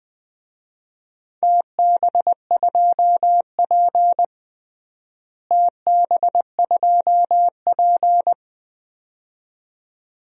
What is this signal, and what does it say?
Morse code, heard as a keyed tone, about 20 wpm, 700 Hz. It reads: TB2P TB2P
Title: DXCC-100 Call Signs - 4 Characters - Repeated 2x - 20wpm